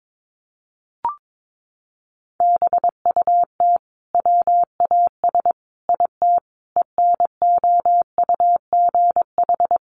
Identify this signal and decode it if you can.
Morse code, heard as a keyed tone, about 22 wpm, 700 Hz. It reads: BUT WAS IT ENOUGH
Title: Sentences from Top 1400 Words 22wpm